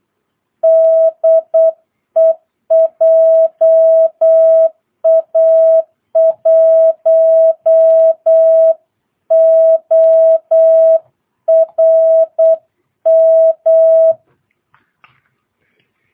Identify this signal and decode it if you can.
Morse code, heard as a keyed tone, about 8 wpm, 660 Hz. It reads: DE JA1ORM
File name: jh8cbh / 2022cw